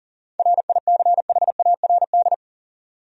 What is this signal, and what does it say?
Morse code, heard as a keyed tone, about 40 wpm, 700 Hz. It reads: RICHARD
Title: USA Common Names - 40wpm